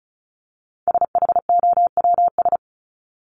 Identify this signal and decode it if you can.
Morse code, heard as a keyed tone, about 34 wpm, 700 Hz. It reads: SHOWS
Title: ES:S Suffix - Top 100 Words with Base Words 35wpm